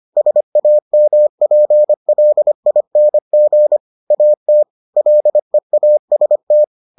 Morse code, heard as a keyed tone, about 25 wpm, 600 Hz. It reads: SAMPLING AT LEAST